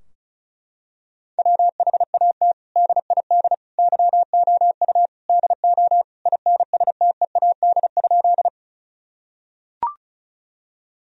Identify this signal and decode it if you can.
Morse code, heard as a keyed tone, about 36 wpm, 700 Hz. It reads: WHAT DID YOU DO INSTEAD?
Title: Sentences from Top 1000 Words - Review Part 4 - 35wpm